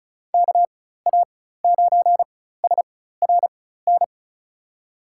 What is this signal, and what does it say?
Morse code, heard as a keyed tone, about 34 wpm, spaced as for 16 wpm, 700 Hz. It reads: KA9SRN
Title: USA Call Signs - 6 Characters - 4x Character Spacing - 35wpm